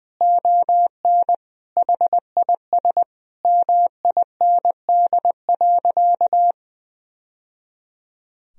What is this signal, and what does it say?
Morse code, heard as a keyed tone, about 20 wpm, 700 Hz. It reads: ON HIS MIND.
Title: Sentences from Top 1000 Words - Review Part 2 - 20wpm